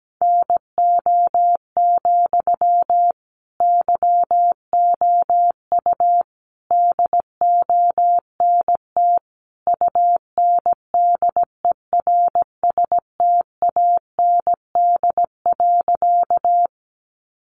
Morse code, heard as a keyed tone, about 17 wpm, 700 Hz. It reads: NO, YOU DONT UNDERSTAND.
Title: Sentences from Top 600 Words 17wpm